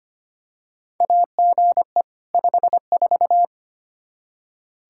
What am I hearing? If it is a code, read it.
Morse code, heard as a keyed tone, about 25 wpm, 700 Hz. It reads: AGE 54
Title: Practice QSO - Exchange 3 - More Info - 25wpm